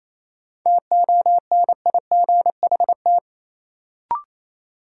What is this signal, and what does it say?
Morse code, heard as a keyed tone, about 28 wpm, 700 Hz. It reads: TONIGHT
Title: Sentences from Top 600 Words - New Word Drill - 28wpm